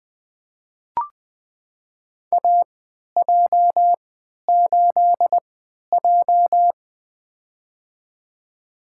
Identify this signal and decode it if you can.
Morse code, heard as a keyed tone, about 20 wpm, spaced as for 11 wpm, 700 Hz. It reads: AJ8J
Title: USA Call Signs - 4 Characters - 3x Character Spacing - 20wpm